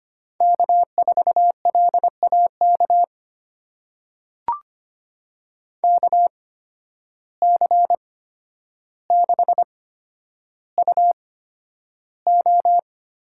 Morse code, heard as a keyed tone, about 25 wpm, spaced as for 11 wpm, 700 Hz. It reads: K4LAK KC6UO
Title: USA Call Signs - 8x Character Spacing - 25wpm